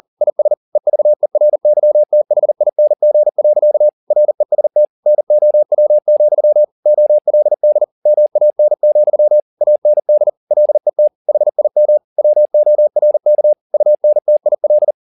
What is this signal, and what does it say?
Morse code, heard as a keyed tone, about 40 wpm, 600 Hz. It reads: IS EVERYTHING. REST NOW, OLD MAN, AND LET HIM WORK UNTIL
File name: the old man and the sea 2 40wpm